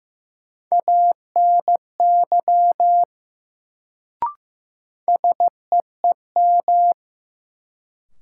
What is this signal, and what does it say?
Morse code, heard as a keyed tone, about 15 wpm, 700 Hz. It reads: ANY SEEM
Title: Sets of 3 Words - New Word Drill - 15wpm